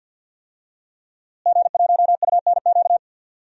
Morse code, heard as a keyed tone, about 53 wpm, 700 Hz. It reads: G1UNQ
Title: DXCC-100 Call Signs - 5 Characters - Repeated 3x - 50wpm